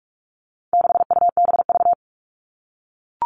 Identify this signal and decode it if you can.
Morse code, heard as a keyed tone, about 44 wpm, 700 Hz. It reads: 6UBV
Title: All Cut Numbers - 4-Digit 45wpm